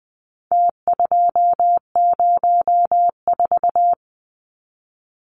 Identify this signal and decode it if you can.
Morse code, heard as a keyed tone, about 20 wpm, 700 Hz. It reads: T204
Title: Numbers - Contest Style - 4-Digit 20wpm